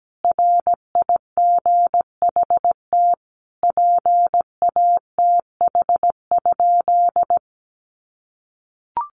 Morse code, heard as a keyed tone, about 17 wpm, 700 Hz. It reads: RIGHT PATH?
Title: Sentences from Top 1400 Words 17wpm